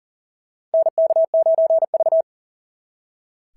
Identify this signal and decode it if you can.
Morse code, heard as a keyed tone, about 40 wpm, 690 Hz. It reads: NK9V
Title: SOTA QSO Practice 40wpm